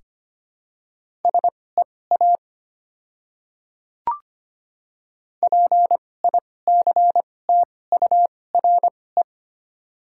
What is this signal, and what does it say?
Morse code, heard as a keyed tone, about 25 wpm, spaced as for 18 wpm, 700 Hz. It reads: SEA PICTURE
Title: 2x Character Spacing - Top 300 Words - 25wpm